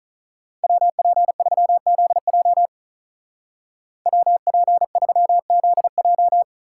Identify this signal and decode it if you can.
Morse code, heard as a keyed tone, about 45 wpm, 700 Hz. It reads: WP3ZJ WP3ZJ